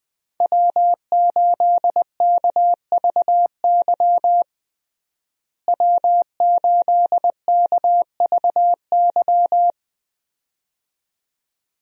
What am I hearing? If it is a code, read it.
Morse code, heard as a keyed tone, about 21 wpm, 700 Hz. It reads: W8KVY W8KVY
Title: USA Call Signs - 5 Characters - Repeated 2x - 20wpm